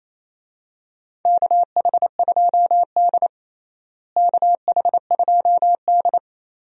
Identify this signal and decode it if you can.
Morse code, heard as a keyed tone, about 28 wpm, 700 Hz. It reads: KH2D KH2D
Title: USA Call Signs - 4 Characters - Repeated 2x - 28wpm